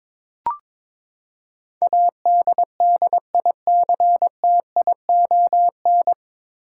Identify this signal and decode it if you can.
Morse code, heard as a keyed tone, about 23 wpm, 700 Hz. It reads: ADDICTION
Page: Morse Code Ninja practice 22wpm